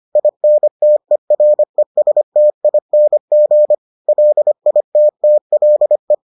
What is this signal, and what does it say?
Morse code, heard as a keyed tone, about 25 wpm, 600 Hz. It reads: INTERESTING LITTLE